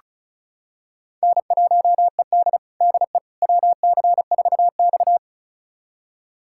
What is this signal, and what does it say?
Morse code, heard as a keyed tone, about 36 wpm, 700 Hz. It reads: N1ED DE WC4X
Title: USA Callsign DE Callsign - 4 Characters - 35wpm